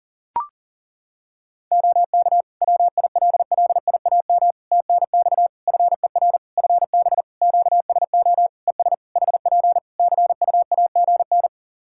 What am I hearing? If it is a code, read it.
Morse code, heard as a keyed tone, about 40 wpm, 700 Hz. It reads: OK WILLIAM TNX FER FB QSO ES HP CUAGN
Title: Practice QSO - Exchange 4 - Ending - 40wpm